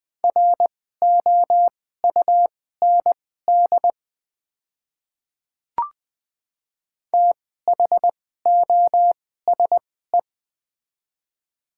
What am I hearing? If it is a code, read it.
Morse code, heard as a keyed tone, about 20 wpm, spaced as for 14 wpm, 700 Hz. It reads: ROUND THOSE
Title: 2x Character Spacing - Top 300 Words - 20wpm